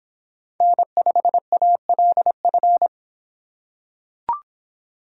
Morse code, heard as a keyed tone, about 24 wpm, 700 Hz. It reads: N5ALF